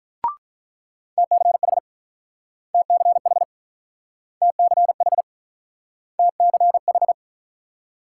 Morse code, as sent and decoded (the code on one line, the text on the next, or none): - -.-. .... / - -.-. .... / - -.-. .... / - -.-. ....
TCH TCH TCH TCH